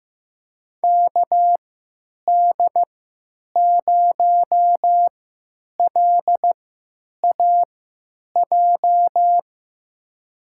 Morse code, heard as a keyed tone, about 15 wpm, spaced as for 9 wpm, 700 Hz. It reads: KD0LAJ
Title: USA Call Signs - 3x Character Spacing - 15wpm